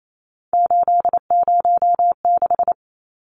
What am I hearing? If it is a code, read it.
Morse code, heard as a keyed tone, about 28 wpm, 700 Hz. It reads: 806
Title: Numbers - 3-Digit 28wpm